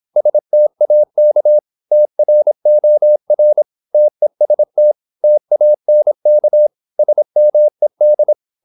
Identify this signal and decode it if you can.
Morse code, heard as a keyed tone, about 26 wpm, 600 Hz. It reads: STAK TROR TEST TANK SMED
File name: Korte ord DK 26wpm